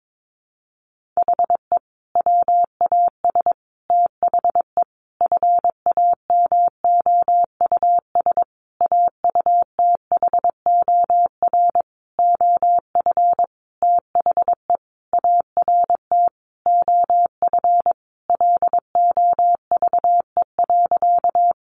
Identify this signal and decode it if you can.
Morse code, heard as a keyed tone, about 22 wpm, 700 Hz. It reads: HE WAS THE FAMOUS AUTHOR OF THE ART OF LOVE.
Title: Sentences from Top 1500 Words 22wpm